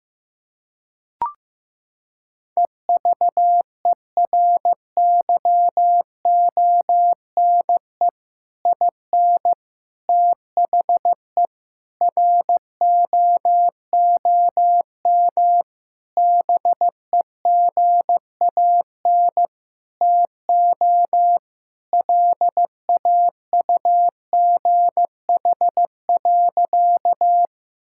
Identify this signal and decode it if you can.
Morse code, heard as a keyed tone, about 15 wpm, 700 Hz. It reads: EVERYONE IN THE ROOM BEGAN TO LAUGH.
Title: Sentences from Top 1000 Words - Review Part 5 - 15wpm